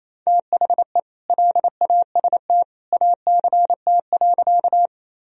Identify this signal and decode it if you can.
Morse code, heard as a keyed tone, about 28 wpm, 700 Hz. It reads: THE LAST ACT.
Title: Sentences from Top 1000 Words - Review Part 1 - 28wpm